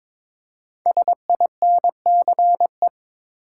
Morse code, heard as a keyed tone, about 22 wpm, 700 Hz. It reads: SINCE
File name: 5 Letter Words - First 3 Letters Given - 22wpm